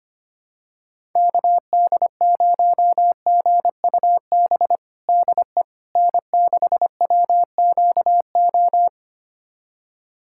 Morse code, heard as a keyed tone, about 25 wpm, 700 Hz. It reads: KD0GUB DE N6WQO
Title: USA Callsign DE Callsign 25wpm